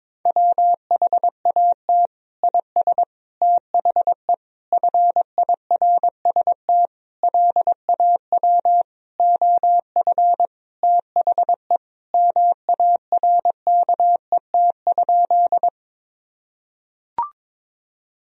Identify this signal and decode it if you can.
Morse code, heard as a keyed tone, about 22 wpm, 700 Hz. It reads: WHAT IS THE FIRST LAW OF THE MARKET?
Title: Sentences from Top 1000 Words - Review Part 3 - 22wpm